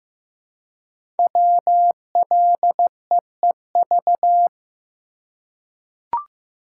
Morse code, heard as a keyed tone, about 15 wpm, 700 Hz. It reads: WLEEV
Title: Code Groups - 5 Letters - 15wpm